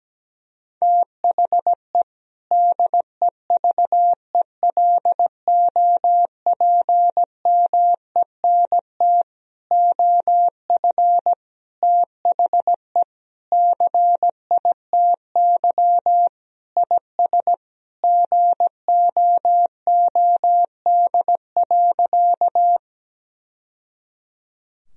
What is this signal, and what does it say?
Morse code, heard as a keyed tone, about 17 wpm, 700 Hz. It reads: THE DEVELOPMENT OF THE CITY IS GOOD.
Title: Sentences from Top 1000 Words - Review Part 4 - 17wpm